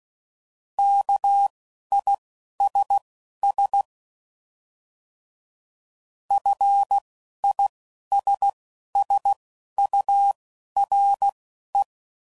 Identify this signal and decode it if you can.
Morse code, heard as a keyed tone, about 16 wpm, spaced as for 11 wpm, 800 Hz. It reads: KISS FISSURE